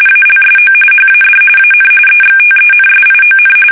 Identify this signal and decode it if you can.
RTTY, about 45.45 baud. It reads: ALSO HAD AN EXTENSION
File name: fsk850 / battleshipnj